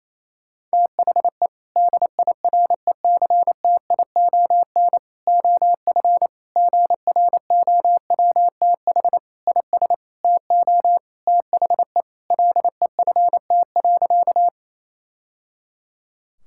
Morse code, heard as a keyed tone, about 28 wpm, 700 Hz. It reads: THE DIRECTION OF GROWTH IS TO THE LEFT.
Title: Sentences from Top 1000 Words - Review Part 5 - 28wpm